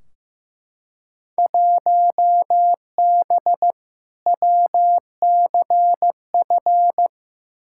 Morse code, heard as a keyed tone, about 15 wpm, 700 Hz. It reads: 1B WCF
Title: Field Day Exchanges Encore - 15wpm